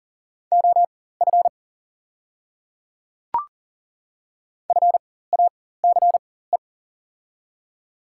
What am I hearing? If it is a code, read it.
Morse code, heard as a keyed tone, about 40 wpm, spaced as for 19 wpm, 700 Hz. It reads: OF FACE
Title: 4x Character Spacing - Top 300 Words - 40wpm